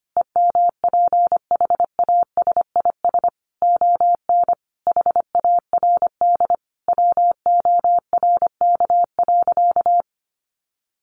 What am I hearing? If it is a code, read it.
Morse code, heard as a keyed tone, about 25 wpm, 700 Hz. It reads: EMPHASIS ON HARD WORK.
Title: Sentences from Top 1400 Words 25wpm